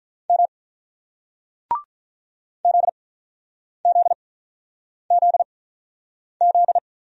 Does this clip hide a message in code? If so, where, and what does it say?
Yes: Morse code from 0.3 to 6.8 s, heard as a keyed tone, about 46 wpm, spaced as for 9 wpm, 690 Hz. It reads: M ZZZZ